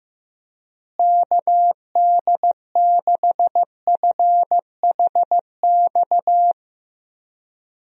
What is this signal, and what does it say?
Morse code, heard as a keyed tone, about 15 wpm, 700 Hz. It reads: KD6FHX
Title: USA Call Signs - 6 Characters - Repeated 2x - 15wpm